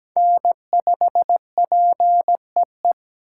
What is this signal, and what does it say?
Morse code, heard as a keyed tone, about 20 wpm, 700 Hz. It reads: N5PEE
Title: Speed Racing - USA 1x3 Call Signs 17wpm